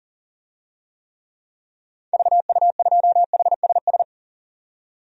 Morse code, heard as a keyed tone, about 37 wpm, 700 Hz. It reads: VU2HSS
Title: DXCC-100 Call Signs - 6 Characters - 40wpm